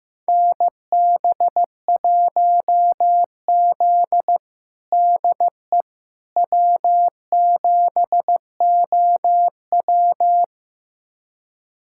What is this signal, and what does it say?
Morse code, heard as a keyed tone, about 16 wpm, 700 Hz. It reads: NB1Z DE W7OW
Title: USA Callsign DE Callsign - 4 Characters - 15wpm